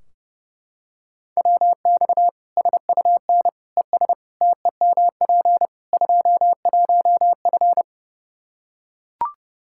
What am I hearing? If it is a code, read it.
Morse code, heard as a keyed tone, about 30 wpm, 700 Hz. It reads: WX SUN ES TEMP 21F
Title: Practice QSO - Exchange 2 - Info - 30wpm